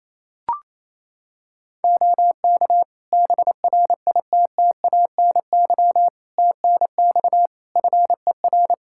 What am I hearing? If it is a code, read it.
Morse code, heard as a keyed tone, about 28 wpm, 700 Hz. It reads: OK BRITTANY TNX FER
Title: Practice QSO - Exchange 4 - Ending - 28wpm